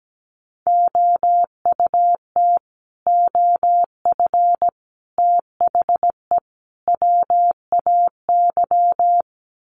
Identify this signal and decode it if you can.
Morse code, heard as a keyed tone, about 17 wpm, 700 Hz. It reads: OUT OF THE WAY
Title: Sets of 4 Words 17wpm